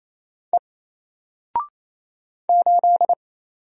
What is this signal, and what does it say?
Morse code, heard as a keyed tone, about 28 wpm, 700 Hz. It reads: E 8
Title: Single Character (Letter:Number:Punctuation:Prosign) Alt 28wpm